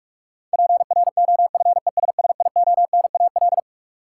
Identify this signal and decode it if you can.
Morse code, heard as a keyed tone, about 46 wpm, 700 Hz. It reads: PROFESSIONAL
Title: Sentences from Top 1000 Words - Review Part 4 - New Word Drill - 45wpm